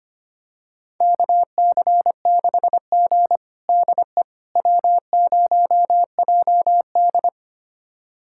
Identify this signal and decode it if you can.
Morse code, heard as a keyed tone, about 25 wpm, 700 Hz. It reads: KC6G DE W0JD
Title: USA Callsign DE Callsign - 4 Characters - 25wpm